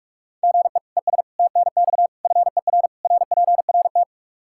Morse code, heard as a keyed tone, about 46 wpm, 700 Hz. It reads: GE ES TNX FER RPRT
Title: K1USN SST QSO Practice 45wpm